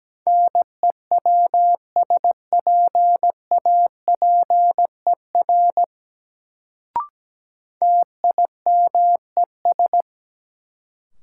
Morse code, heard as a keyed tone, about 17 wpm, 700 Hz. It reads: NEWSPAPER TIMES
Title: Sentences from Top 1100 Words - New Word Drill Alt - 17wpm